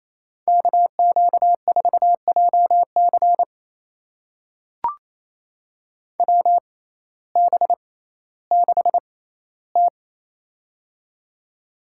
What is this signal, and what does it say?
Morse code, heard as a keyed tone, about 28 wpm, spaced as for 15 wpm, 700 Hz. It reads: KQ4JC WB6T
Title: USA Call Signs - 6x Character Spacing - 28wpm